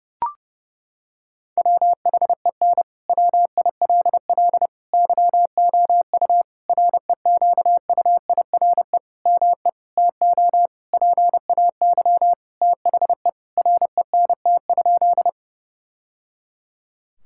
Morse code, heard as a keyed tone, about 30 wpm, 700 Hz. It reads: WHEN WILL YOU REQUIRE ME TO PAY THE RENT?
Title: Sentences from Top 1000 Words - Review Part 6 - 30wpm